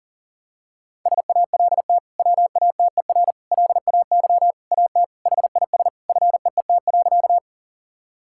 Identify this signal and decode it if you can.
Morse code, heard as a keyed tone, about 40 wpm, 700 Hz. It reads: SALT WATER LAY AT HIS FEET.
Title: Sentences from Top 1000 Words - Review Part 5 - 40wpm